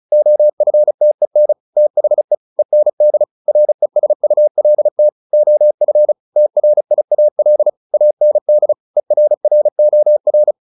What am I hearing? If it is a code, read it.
Morse code, heard as a keyed tone, about 35 wpm, 600 Hz. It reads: OFTEN THE END RESULT OF TRIAL AND ERROR